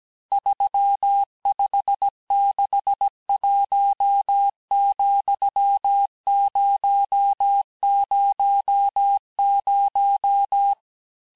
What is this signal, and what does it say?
Morse code, heard as a keyed tone, about 17 wpm, 800 Hz. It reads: 3561,000
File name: cours-du-lundi files / corrige-cours-1581-texte-chiffre-vitesse-16-mpm